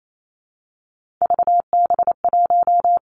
Morse code, heard as a keyed tone, about 28 wpm, 700 Hz. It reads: VB1
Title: Three Challenging Letter-Number Combinations 28wpm